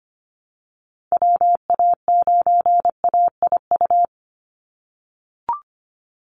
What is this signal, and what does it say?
Morse code, heard as a keyed tone, about 25 wpm, 700 Hz. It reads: WA9AIU
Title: USA Call Signs 25wpm